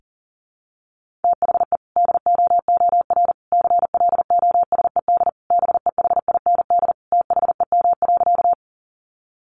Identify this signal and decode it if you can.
Morse code, heard as a keyed tone, about 40 wpm, 700 Hz. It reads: THE DOOR CLOSED BEHIND THEM.
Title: Sentences from Top 1400 Words 40wpm